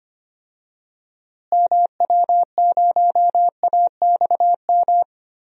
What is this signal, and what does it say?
Morse code, heard as a keyed tone, about 25 wpm, 700 Hz. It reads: MW0AXM
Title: DXCC-100 Call Signs - 6 Characters - Repeated 2x - 25wpm